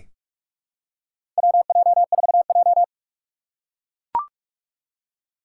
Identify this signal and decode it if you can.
Morse code, heard as a keyed tone, about 44 wpm, 700 Hz. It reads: WJ4J
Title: USA Call Signs - 4 Characters - 6x Character Spacing - 45wpm